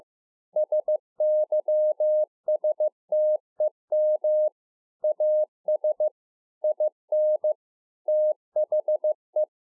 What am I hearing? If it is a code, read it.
Morse code, heard as a keyed tone, about 15 wpm, 610 Hz. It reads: SYSTEM AS IN THE
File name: 15WPM